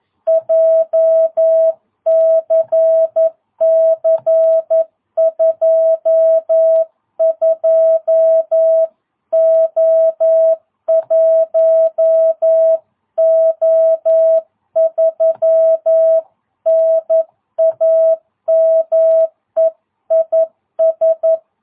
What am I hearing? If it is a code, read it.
Morse code, heard as a keyed tone, about 11 wpm, 650 Hz. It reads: JCC 22O1O3 NAME IS
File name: jh8cbh / 2022cw